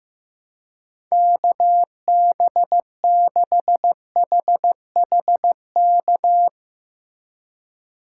Morse code, heard as a keyed tone, about 15 wpm, 700 Hz. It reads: KB6HHK
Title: USA Call Signs - 2x Character Spacing - 15wpm